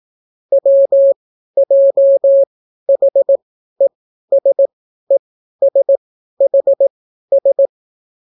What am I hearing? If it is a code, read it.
Morse code, heard as a keyed tone, about 18 wpm, spaced as for 12 wpm, 550 Hz. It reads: WJHESESHS